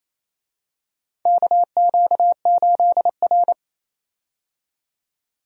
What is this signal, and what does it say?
Morse code, heard as a keyed tone, about 28 wpm, 700 Hz. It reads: KQ8R
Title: USA Call Signs - 4 Characters - 28wpm